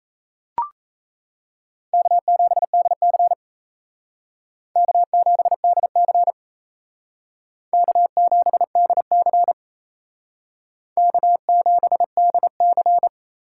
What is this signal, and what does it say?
Morse code, heard as a keyed tone, about 36 wpm, 700 Hz. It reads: K7DC K7DC K7DC K7DC